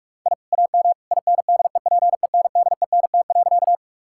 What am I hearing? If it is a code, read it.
Morse code, heard as a keyed tone, about 45 wpm, 700 Hz. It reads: I AM INDEPENDENT.
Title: Sentences from Top 1000 Words - Review Part 6 - 45wpm